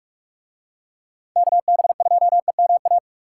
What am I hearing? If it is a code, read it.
Morse code, heard as a keyed tone, about 44 wpm, 700 Hz. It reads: KB2EMA